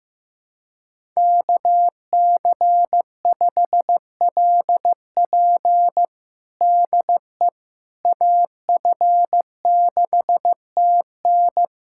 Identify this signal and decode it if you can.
Morse code, heard as a keyed tone, about 15 wpm, 700 Hz. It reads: KC5LP DE AF6TN
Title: USA Callsign DE Callsign - 5 Characters - 15wpm